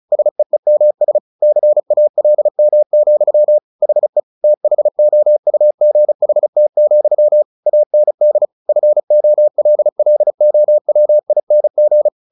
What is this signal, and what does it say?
Morse code, heard as a keyed tone, about 34 wpm, 600 Hz. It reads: SEEMS CALM, HE THOUGHT, AND FOLLOWING
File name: the old man and the sea / the old man and the sea 1 35wpm